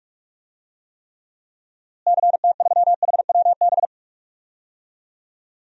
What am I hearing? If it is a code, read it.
Morse code, heard as a keyed tone, about 42 wpm, 700 Hz. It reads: CT3HWB
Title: DXCC-100 Call Signs - 6 Characters - Repeated 3x - 45wpm